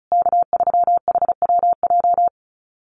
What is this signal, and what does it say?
Morse code, heard as a keyed tone, about 34 wpm, 700 Hz. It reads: K3HWJ
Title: USA Call Signs Repeated 3x 35wpm